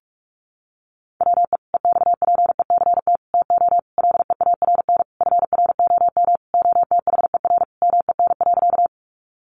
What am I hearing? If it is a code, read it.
Morse code, heard as a keyed tone, about 44 wpm, 700 Hz. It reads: WE EXPECT TO LEARN FROM OTHER MEN.
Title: Sentences from Top 1100 Words 45wpm